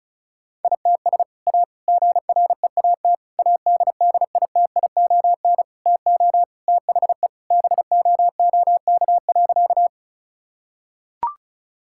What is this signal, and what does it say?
Morse code, heard as a keyed tone, about 34 wpm, 700 Hz. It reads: ITS A GREAT ADDITION TO THE BOOK.
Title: Sentences from Top 1000 Words - Review Part 6 - 35wpm